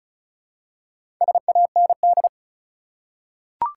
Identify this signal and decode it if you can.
Morse code, heard as a keyed tone, about 34 wpm, 700 Hz. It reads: SAND
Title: Four Letter Words - 35wpm